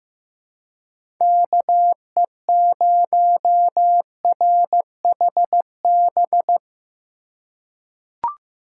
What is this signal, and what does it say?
Morse code, heard as a keyed tone, about 15 wpm, 700 Hz. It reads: KE0RHB
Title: USA Call Signs - 6 Characters - 2x Character Spacing - 15wpm